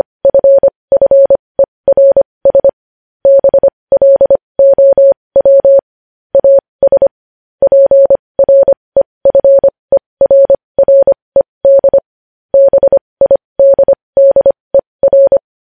Morse code, heard as a keyed tone, about 25 wpm, spaced as for 20 wpm, 550 Hz. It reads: FFERS BLOW AS PREFERRED BIDDER
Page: news-headlines-in-morse-code-at-20-wpm